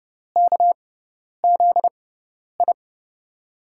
Morse code, heard as a keyed tone, about 30 wpm, spaced as for 10 wpm, 700 Hz. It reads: KZI
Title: USA Call Signs - 6x Character Spacing - 30wpm